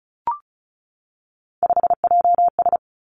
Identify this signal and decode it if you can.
Morse code, heard as a keyed tone, about 36 wpm, 700 Hz. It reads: 5JS
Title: Three Challenging Letter-Number Combinations 35wpm